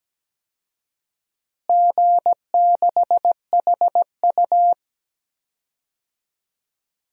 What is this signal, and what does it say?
Morse code, heard as a keyed tone, about 17 wpm, 700 Hz. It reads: G6HU
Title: DXCC-100 Call Signs - 4 Characters - Repeated 2x - 17wpm